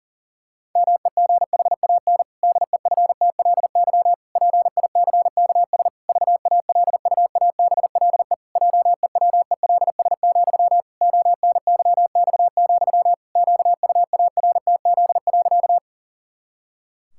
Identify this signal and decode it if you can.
Morse code, heard as a keyed tone, about 40 wpm, 700 Hz. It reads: MEGHAN DEFTLY PICKS VALUABLE JEWELS, ONYX, QUARTZ.
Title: Pangram Challenge v2 - 40wpm